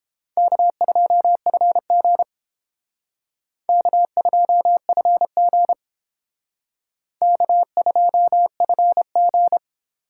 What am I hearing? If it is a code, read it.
Morse code, heard as a keyed tone, about 28 wpm, 700 Hz. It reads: K2FG K2FG K2FG